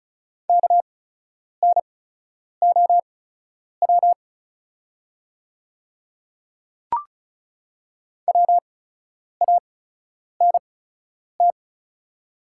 Morse code, heard as a keyed tone, about 36 wpm, spaced as for 10 wpm, 700 Hz. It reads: KNOW WANT
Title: 8x Character Spacing - Top 300 Words - 35wpm